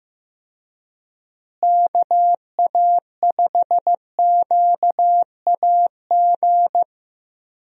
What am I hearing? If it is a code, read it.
Morse code, heard as a keyed tone, about 15 wpm, 700 Hz. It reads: KA5QAG
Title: USA Call Signs - 6 Characters - Repeated 3x - 15wpm